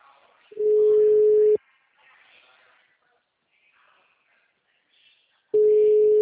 Ringback tone (Europe, 425 Hz).
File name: out- -2013- - - .5958